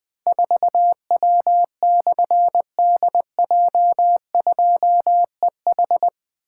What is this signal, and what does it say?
Morse code, heard as a keyed tone, about 20 wpm, 700 Hz. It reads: 4W/DJ2EH
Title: Difficult Call Signs 20wpm